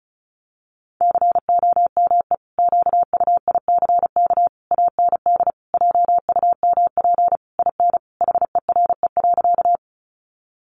Morse code, heard as a keyed tone, about 34 wpm, 700 Hz. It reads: COME QUICK AND JUMP IN HERE.
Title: Sentences from Top 1300 Words 35wpm